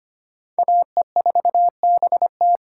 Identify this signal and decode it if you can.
Morse code, heard as a keyed tone, about 25 wpm, 700 Hz. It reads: AE4BT
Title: USA Call Signs - 5 Characters - 4x Character Spacing - 25wpm